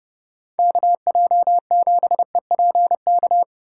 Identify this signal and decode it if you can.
Morse code, heard as a keyed tone, about 29 wpm, 700 Hz. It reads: KJ7EPK